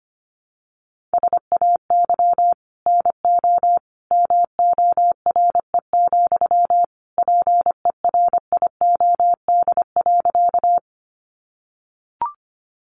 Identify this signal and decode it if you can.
Morse code, heard as a keyed tone, about 25 wpm, 700 Hz. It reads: SAY NO MORE, PERIOD.
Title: Sentences from Top 800 Words 25wpm